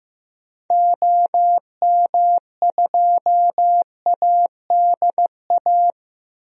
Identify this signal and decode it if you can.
Morse code, heard as a keyed tone, about 15 wpm, 700 Hz. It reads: OM2ADA
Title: DXCC-100 Call Signs - 6 Characters - Repeated 3x - 15wpm